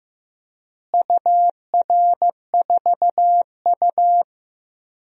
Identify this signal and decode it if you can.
Morse code, heard as a keyed tone, about 15 wpm, 700 Hz. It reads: UR4U
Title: DXCC-100 Call Signs - 4 Characters - Repeated 2x - 15wpm